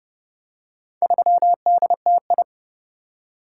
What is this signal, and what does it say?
Morse code, heard as a keyed tone, about 28 wpm, 700 Hz. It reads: 3DTI